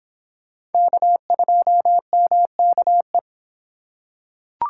Morse code, heard as a keyed tone, about 26 wpm, 700 Hz. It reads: K2MKE